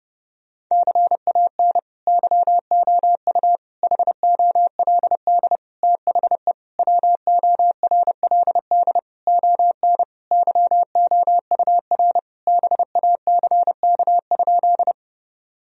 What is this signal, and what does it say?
Morse code, heard as a keyed tone, about 32 wpm, 700 Hz. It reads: CAN YOU HOLD THE WORLD ON YOUR BACK?
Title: Sentences from Top 1000 Words - Review Part 2 - 30wpm